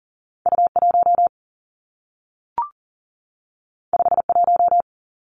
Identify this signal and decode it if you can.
Morse code, heard as a keyed tone, about 39 wpm, 700 Hz. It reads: U 1 5 1
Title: Two Challenging Letter-Number Combinations 40wpm